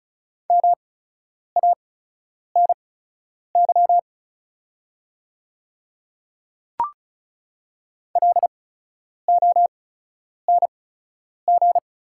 Morse code, heard as a keyed tone, about 35 wpm, spaced as for 10 wpm, 690 Hz. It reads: MANY LONG
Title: 8x Character Spacing - Top 100 Words - 35wpm